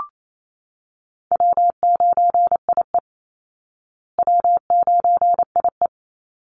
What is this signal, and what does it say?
Morse code, heard as a keyed tone, about 28 wpm, 700 Hz. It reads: W9IE W9IE
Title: Easy Call Signs Repeated 3x - 28wpm